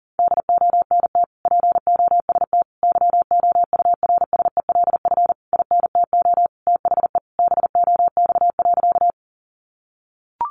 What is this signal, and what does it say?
Morse code, heard as a keyed tone, about 40 wpm, 700 Hz. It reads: DONT POST YOURSELF INTO THE BOX.
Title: Sentences from Top 1000 Words 40wpm